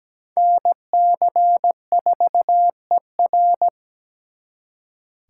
Morse code, heard as a keyed tone, about 20 wpm, 700 Hz. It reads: NC4ER
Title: Speed Racing - USA Call Signs 17wpm